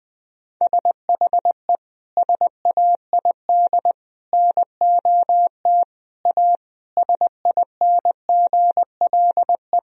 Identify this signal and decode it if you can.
Morse code, heard as a keyed tone, about 21 wpm, 700 Hz. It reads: SHE SAID NOT A SINGLE
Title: Sentences from Top 1000 Words - Review Part 4 - 20wpm